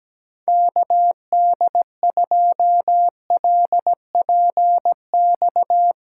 Morse code, heard as a keyed tone, about 17 wpm, 700 Hz. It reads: KD2LPX
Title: USA Callsign DE Callsign - 6 Characters - 17wpm